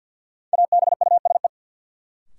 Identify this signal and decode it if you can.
Morse code, heard as a keyed tone, about 53 wpm, 700 Hz. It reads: ABUSE
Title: Sentences from Top 1300 Words - New Word Drill Alt - 50wpm